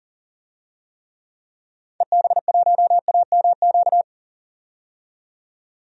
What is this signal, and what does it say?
Morse code, heard as a keyed tone, about 40 wpm, 700 Hz. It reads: EB1AMQ
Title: DXCC-100 Call Signs - 6 Characters - 40wpm